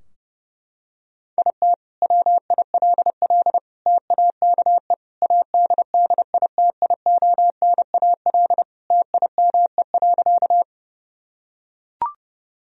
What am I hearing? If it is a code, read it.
Morse code, heard as a keyed tone, about 28 wpm, 700 Hz. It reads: IT WILL TAKE ADDITIONAL TIME.
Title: Sentences from Top 1000 Words - Review Part 4 - 30wpm